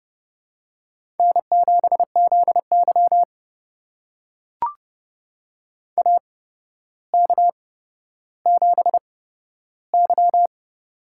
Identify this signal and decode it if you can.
Morse code, heard as a keyed tone, about 30 wpm, spaced as for 13 wpm, 700 Hz. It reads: N7ZY AK7Y
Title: USA Call Signs - 4 Characters - 8x Character Spacing - 30wpm